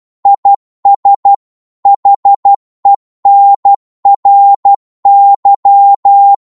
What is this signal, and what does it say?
Morse code, heard as a keyed tone, about 12 wpm, 800 Hz. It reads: IS HENRY
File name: Morse / qso-12wpm